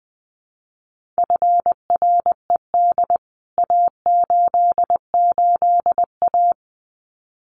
Morse code, heard as a keyed tone, about 20 wpm, 700 Hz. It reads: FRED A88A
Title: CWT Exchange 20wpm